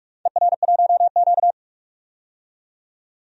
Morse code, heard as a keyed tone, about 46 wpm, 700 Hz. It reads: ER1Q
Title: DXCC-100 Call Signs - 4 Characters - 45wpm